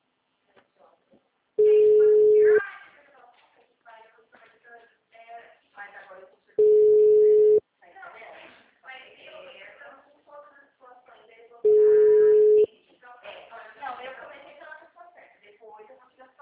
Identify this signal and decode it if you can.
Ringback tone (Europe, 425 Hz), with speech.
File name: out- -2020- - - .7